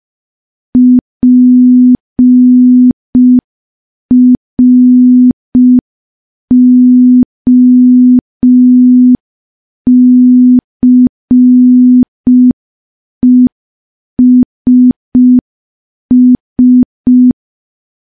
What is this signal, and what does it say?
Morse code, heard as a keyed tone, about 5 wpm, 250 Hz. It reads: PROCESS
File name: morse / morse-training-05WPM